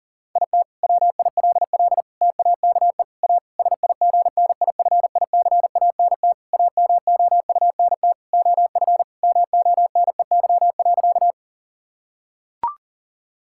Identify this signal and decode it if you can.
Morse code, heard as a keyed tone, about 40 wpm, 700 Hz. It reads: IT WILL TAKE A SIGNIFICANT AMOUNT OF MONEY.
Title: Sentences from Top 1000 Words - Review Part 6 - 40wpm